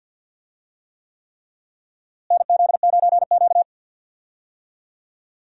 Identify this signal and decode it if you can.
Morse code, heard as a keyed tone, about 52 wpm, 700 Hz. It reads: NZ9Q
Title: USA Call Signs - 4 Characters - Repeated 3x - 50wpm